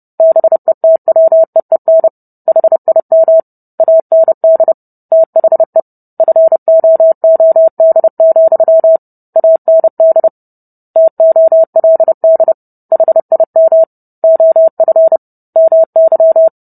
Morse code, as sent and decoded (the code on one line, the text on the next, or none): -... - .-- . -. / .... .. -- / .- -. -.. / - .... / ..-. --- --- -.. --..-- / .- -. -.. / - --- .-.. -.. / .... .. -- / --- ..-. / -- -.--
BETWEEN HIM AND THE FOOD, AND TOLD HIM OF MY